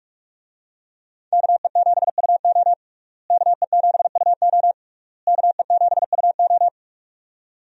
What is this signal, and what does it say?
Morse code, heard as a keyed tone, about 46 wpm, 700 Hz. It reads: KE7UO KE7UO KE7UO
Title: USA Call Signs - 5 Characters - Repeated 3x - 45wpm